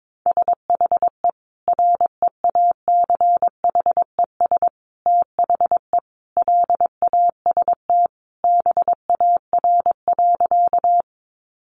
Morse code, heard as a keyed tone, about 22 wpm, 700 Hz. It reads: SHE REACHES THE LAST BAR.
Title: Easy Phrases 22wpm